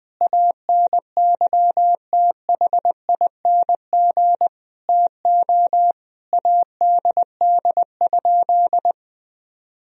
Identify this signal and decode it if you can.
Morse code, heard as a keyed tone, about 21 wpm, 700 Hz. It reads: ANYTHING TO ADD?
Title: Sentences from Top 1000 Words - Review Part 1 - 20wpm